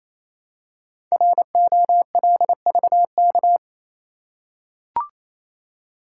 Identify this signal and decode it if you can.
Morse code, heard as a keyed tone, about 29 wpm, 700 Hz. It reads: ROLVK
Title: Code Groups - 5 Letter:Numbers - 28wpm